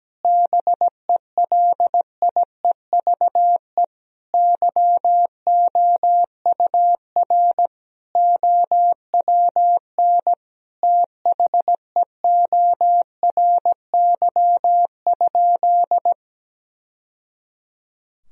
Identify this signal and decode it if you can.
Morse code, heard as a keyed tone, about 17 wpm, 700 Hz. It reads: BELIEVE YOUR OWN THEORY?
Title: Sentences from Top 1000 Words - Review Part 3 - 17wpm